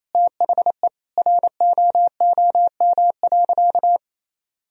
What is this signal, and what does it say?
Morse code, heard as a keyed tone, about 28 wpm, 700 Hz. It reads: THE ROOM.
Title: Sentences from Top 500 Words and Abbreviations 28wpm